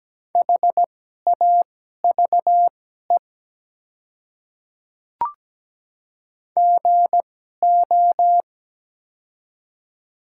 Morse code, heard as a keyed tone, about 17 wpm, spaced as for 12 wpm, 700 Hz. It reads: HAVE GO
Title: 2x Character Spacing - Top 100 Words - 17wpm